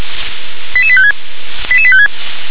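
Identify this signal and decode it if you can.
Five-tone selective call (ZVEI), heard as quick runs of five tones, tones of about 70 ms. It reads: 88856 88856
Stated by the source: Received by SDR; ZVEI five-tone code 88151.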